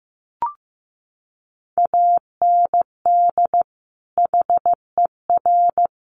Morse code, heard as a keyed tone, about 15 wpm, 700 Hz. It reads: AND HER
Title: Sets of 2 Words 15wpm